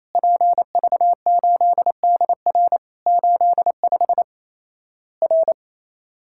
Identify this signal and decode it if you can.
Morse code, heard as a keyed tone, about 28 wpm, 700 Hz. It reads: PV8DR 85 R
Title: Worked All Europe DX Contest - QTC Exchange - 28wpm